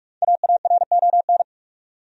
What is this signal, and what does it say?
Morse code, heard as a keyed tone, about 43 wpm, 700 Hz. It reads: AARON
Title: USA Common Names - 45wpm